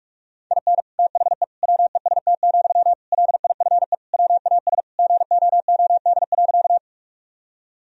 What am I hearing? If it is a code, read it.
Morse code, heard as a keyed tone, about 41 wpm, 700 Hz. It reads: IN THE WEST, LIFE WAS GOOD.
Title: Sentences from Top 1000 Words - Review Part 2 - 45wpm